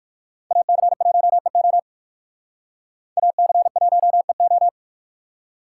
Morse code, heard as a keyed tone, about 56 wpm, 700 Hz. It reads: AC1EO AC1EO